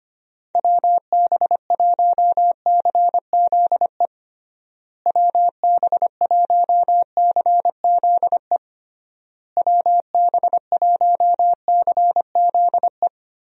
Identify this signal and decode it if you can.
Morse code, heard as a keyed tone, about 25 wpm, 700 Hz. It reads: WB1CZE WB1CZE WB1CZE
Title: USA Call Signs - 6 Characters - Repeated 3x - 25wpm